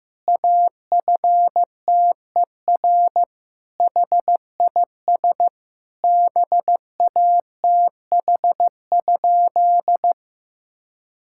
Morse code, heard as a keyed tone, about 16 wpm, 700 Hz. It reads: AFTER HIS BATH?
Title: Sentences from Top 1600 Words 15wpm